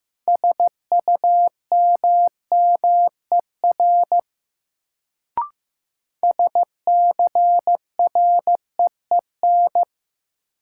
Morse code, heard as a keyed tone, about 15 wpm, 700 Hz. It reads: SUMMER SCREEN
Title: Sentences from Top 1000 Words - Review Part 6 - New Word Drill - 15wpm